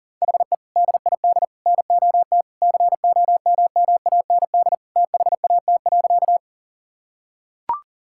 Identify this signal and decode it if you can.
Morse code, heard as a keyed tone, about 40 wpm, 700 Hz. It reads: HE DID NOT COMMAND THAT.
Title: Sentences from Top 1600 Words 40wpm